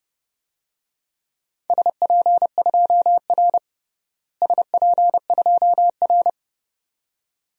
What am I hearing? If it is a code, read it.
Morse code, heard as a keyed tone, about 30 wpm, 700 Hz. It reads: SP2R SP2R
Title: DXCC-100 Call Signs - 4 Characters - Repeated 2x - 30wpm